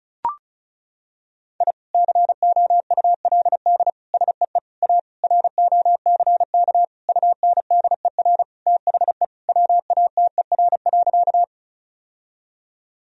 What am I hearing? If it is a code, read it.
Morse code, heard as a keyed tone, about 36 wpm, 700 Hz. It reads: I COULD SEE A ROCK UNDER THE WATER.
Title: Sentences from Top 1000 Words - Review Part 2 - 35wpm